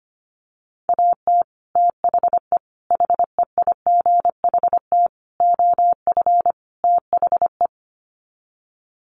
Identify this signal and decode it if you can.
Morse code, heard as a keyed tone, about 25 wpm, 700 Hz. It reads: AT THE HEIGHT OF THE
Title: Sets of 5 Words 25wpm